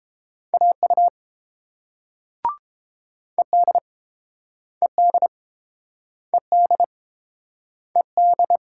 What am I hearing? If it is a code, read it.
Morse code, heard as a keyed tone, about 29 wpm, 700 Hz. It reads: AU ED ED ED ED